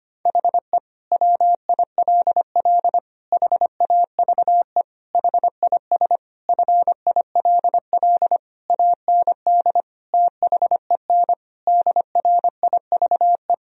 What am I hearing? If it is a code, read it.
Morse code, heard as a keyed tone, about 25 wpm, 700 Hz. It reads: HE WILL HAVE HIS FILL AND THEN DRIVE
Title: Sentences from Top 1000 Words - Review Part 3 - 25wpm